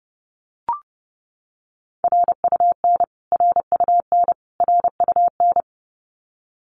Morse code, heard as a keyed tone, about 30 wpm, 700 Hz. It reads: RUN RUN RUN
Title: Top 500 Words Repeated 30wpm